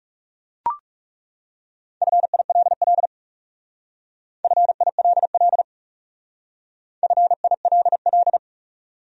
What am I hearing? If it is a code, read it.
Morse code, heard as a keyed tone, about 37 wpm, 700 Hz. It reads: FILL FILL FILL
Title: Speed-Racing Top 500 Words 30wpm